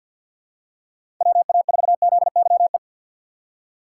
Morse code, heard as a keyed tone, about 47 wpm, 700 Hz. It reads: WA4ZYE
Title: USA Call Signs - 6 Characters - 4x Character Spacing - 50wpm